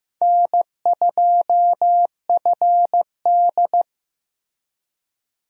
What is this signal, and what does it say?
Morse code, heard as a keyed tone, about 15 wpm, 700 Hz. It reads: N2FD
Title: USA Call Signs - 4 Characters - Repeated 3x - 15wpm